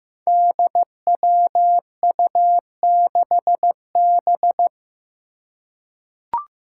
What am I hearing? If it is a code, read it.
Morse code, heard as a keyed tone, about 15 wpm, 700 Hz. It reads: DWU6B
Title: Five Challenging Letter-Number Combinations 15wpm